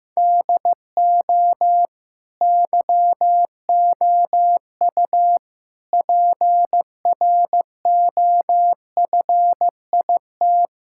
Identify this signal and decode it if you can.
Morse code, heard as a keyed tone, about 15 wpm, 700 Hz. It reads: DO YOU PROFIT
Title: Sentences from Top 1000 Words - Review Part 5 - 15wpm